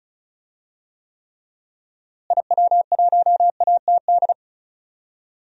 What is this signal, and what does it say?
Morse code, heard as a keyed tone, about 36 wpm, 700 Hz. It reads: IW1ATD
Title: DXCC-100 Call Signs - 6 Characters - Repeated 3x - 35wpm